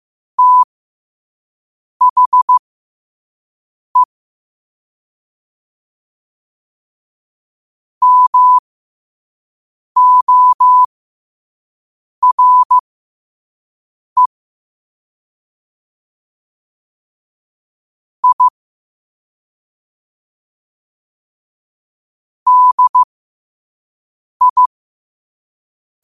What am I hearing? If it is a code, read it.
Morse code, heard as a keyed tone, about 15 wpm, spaced as for 5 wpm, 1000 Hz. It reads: THE MORE I DI